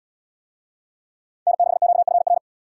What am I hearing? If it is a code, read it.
Morse code, heard as a keyed tone, about 77 wpm, 700 Hz. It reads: N6QFR